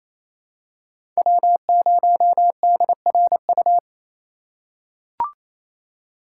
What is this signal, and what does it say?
Morse code, heard as a keyed tone, about 28 wpm, 700 Hz. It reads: W0DRU
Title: USA Call Signs - 5 Characters - 4x Character Spacing - 28wpm